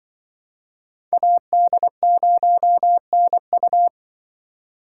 Morse code, heard as a keyed tone, about 25 wpm, 700 Hz. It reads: AD0NU
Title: Speed Racing - USA 2x2 Call Signs 20wpm